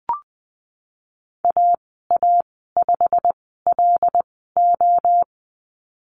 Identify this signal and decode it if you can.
Morse code, heard as a keyed tone, about 20 wpm, spaced as for 14 wpm, 700 Hz. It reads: AA5LO
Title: 088 Lesson 15g - Number 5 - Call Signs 20wpm